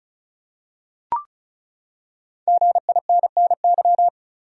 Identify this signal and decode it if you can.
Morse code, heard as a keyed tone, about 36 wpm, 700 Hz. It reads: GINNY
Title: USA Common Names - 35wpm